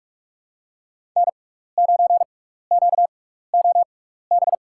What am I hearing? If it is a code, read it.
Morse code, heard as a keyed tone, about 46 wpm, spaced as for 16 wpm, 700 Hz. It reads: N9QOB
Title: USA Call Signs - 5 Characters - 6x Character Spacing - 45wpm